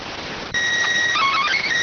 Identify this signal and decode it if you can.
SSTV (slow-scan television), PD 290.